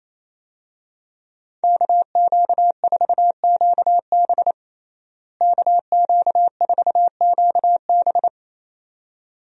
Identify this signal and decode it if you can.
Morse code, heard as a keyed tone, about 28 wpm, 700 Hz. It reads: KQ4QB KQ4QB
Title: USA Call Signs - 5 Characters - Repeated 2x - 28wpm